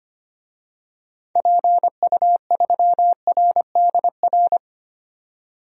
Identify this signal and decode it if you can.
Morse code, heard as a keyed tone, about 25 wpm, 700 Hz. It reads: PU3RDR
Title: DXCC-100 Call Signs - 6 Characters - 25wpm